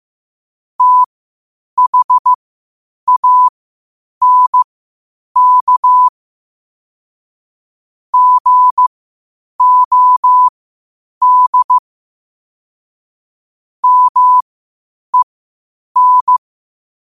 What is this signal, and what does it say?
Morse code, heard as a keyed tone, about 15 wpm, spaced as for 8 wpm, 1000 Hz. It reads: THANK GOD MEN
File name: qotdfiles / 8wpm